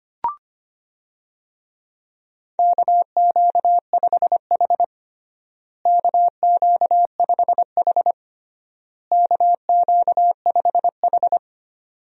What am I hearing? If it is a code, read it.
Morse code, heard as a keyed tone, about 25 wpm, 700 Hz. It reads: KQ5H KQ5H KQ5H
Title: USA Call Signs - 4 Characters - Repeated 3x - 25wpm